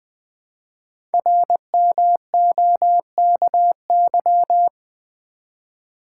Morse code, heard as a keyed tone, about 20 wpm, 700 Hz. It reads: RMOKY
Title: Code Groups - 5 Letters - 20wpm